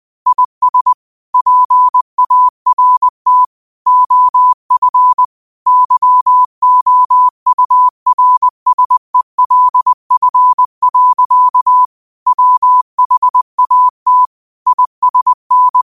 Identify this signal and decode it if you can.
Morse code, heard as a keyed tone, about 20 wpm, 1000 Hz. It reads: IS PART OF YOURSELF. WHAT ISN
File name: 20wpm